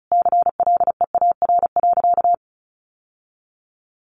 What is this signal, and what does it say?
Morse code, heard as a keyed tone, about 34 wpm, 700 Hz. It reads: CLEAR.
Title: Sentences from Top 1100 Words 35wpm